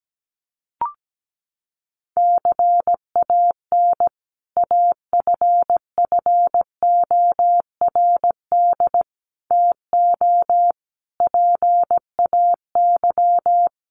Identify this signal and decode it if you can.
Morse code, heard as a keyed tone, about 17 wpm, 700 Hz. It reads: CAN AFFORD TO PAY
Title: Sentences from Top 1100 Words 17wpm